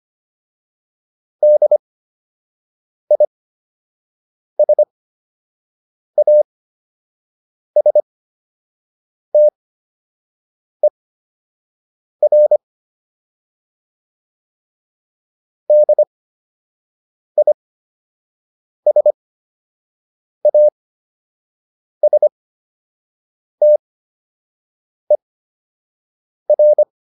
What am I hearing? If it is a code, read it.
Morse code, heard as a keyed tone, about 25 wpm, spaced as for 6 wpm, 600 Hz. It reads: DISASTER DISASTER
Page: fundamental-practice-files